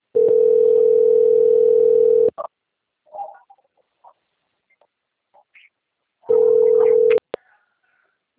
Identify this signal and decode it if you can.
Ringback tone (North America), 440 + 480 Hz.
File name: out- - - - - .26499